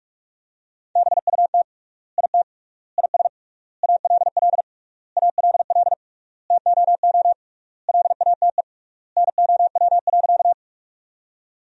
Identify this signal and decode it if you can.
Morse code, heard as a keyed tone, about 42 wpm, 700 Hz. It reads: BUT IT IS ALL ALL TOO LATE NOW.
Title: 3x Word-Spacing - Sentences from Top 300 Words - 45wpm